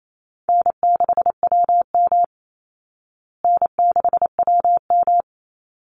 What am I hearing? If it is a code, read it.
Morse code, heard as a keyed tone, about 28 wpm, 700 Hz. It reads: N6WM N6WM
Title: Call Signs Repeated 3x - 28wpm